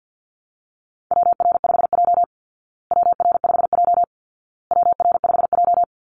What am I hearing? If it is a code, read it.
Morse code, heard as a keyed tone, about 48 wpm, 700 Hz. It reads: WR5J WR5J WR5J
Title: Call Signs Repeated 2x - 50wpm